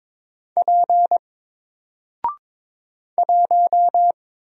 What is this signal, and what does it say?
Morse code, heard as a keyed tone, about 22 wpm, 700 Hz. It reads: P 1
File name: Single-Letter-Number-22wpm